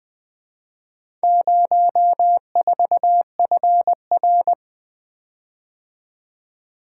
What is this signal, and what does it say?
Morse code, heard as a keyed tone, about 24 wpm, 700 Hz. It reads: 04FR
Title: Speed-Racing - Four Letter-Number Combinations - 20wpm